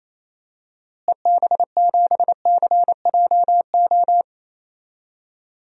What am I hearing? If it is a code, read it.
Morse code, heard as a keyed tone, about 27 wpm, 700 Hz. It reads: EB7CJO
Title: DXCC-100 Call Signs - 6 Characters - Repeated 3x - 28wpm